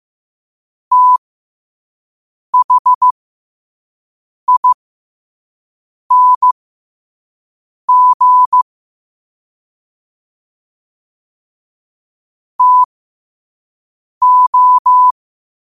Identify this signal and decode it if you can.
Morse code, heard as a keyed tone, about 15 wpm, spaced as for 5 wpm, 1000 Hz. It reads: THING TO
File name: qotdfiles / 5wpm